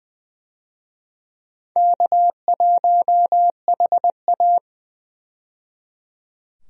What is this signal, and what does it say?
Morse code, heard as a keyed tone, about 20 wpm, 700 Hz. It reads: K1HA
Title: USA Call Signs - 4 Characters - 20wpm